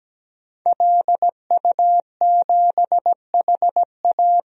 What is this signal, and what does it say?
Morse code, heard as a keyed tone, about 17 wpm, 700 Hz. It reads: LU7HA
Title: DXCC-100 Call Signs - 5 Characters - 17wpm